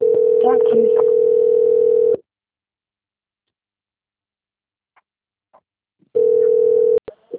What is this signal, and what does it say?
Ringback tone (North America), 440 + 480 Hz, with speech.
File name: out- - - - - .52111